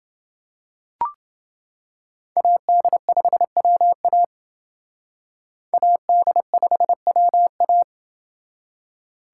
Morse code, heard as a keyed tone, about 28 wpm, 700 Hz. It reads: AD5WA AD5WA